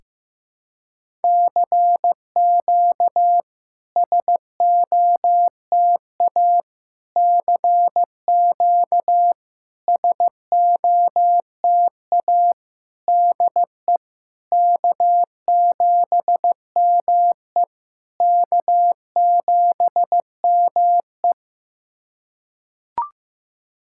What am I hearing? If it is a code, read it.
Morse code, heard as a keyed tone, about 15 wpm, 700 Hz. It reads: CQ SOTA CQ SOTA DE K7ME K7ME
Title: SOTA QSO Practice 15wpm